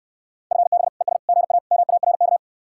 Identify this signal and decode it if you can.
Morse code, heard as a keyed tone, about 69 wpm, 700 Hz. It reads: FB ES GA MARK